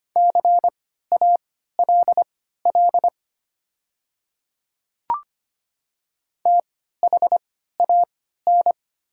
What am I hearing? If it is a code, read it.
Morse code, heard as a keyed tone, about 25 wpm, spaced as for 14 wpm, 700 Hz. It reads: CALL THAN
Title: 3x Character Spacing - Top 100 Words - 25wpm